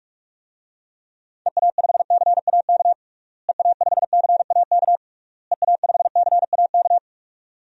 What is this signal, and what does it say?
Morse code, heard as a keyed tone, about 41 wpm, 700 Hz. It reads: EA5CAK EA5CAK EA5CAK
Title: DXCC-100 Call Signs - 6 Characters - Repeated 3x - 45wpm